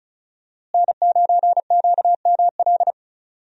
Morse code, heard as a keyed tone, about 36 wpm, 700 Hz. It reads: N9QML
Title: USA Call Signs - 5 Characters - Repeated 2x - 35wpm